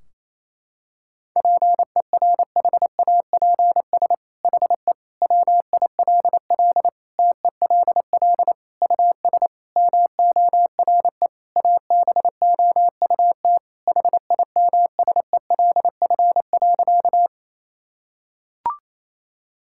Morse code, heard as a keyed tone, about 28 wpm, 700 Hz. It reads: PERHAPS HE WILL TELL US MORE ABOUT HIMSELF.
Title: Sentences from Top 1600 Words 28wpm